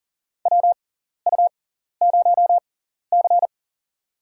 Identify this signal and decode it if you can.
Morse code, heard as a keyed tone, about 43 wpm, spaced as for 14 wpm, 700 Hz. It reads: WU0C
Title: USA Call Signs - 4 Characters - 6x Character Spacing - 40wpm